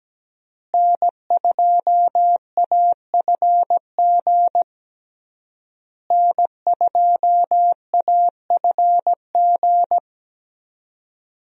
Morse code, heard as a keyed tone, about 17 wpm, 700 Hz. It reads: N2AFG N2AFG
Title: USA Call Signs - 5 Characters - Repeated 3x - 17wpm